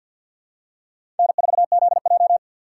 Morse code, heard as a keyed tone, about 52 wpm, 700 Hz. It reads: N4ZJ
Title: USA Call Signs - 6x Character Spacing - 50wpm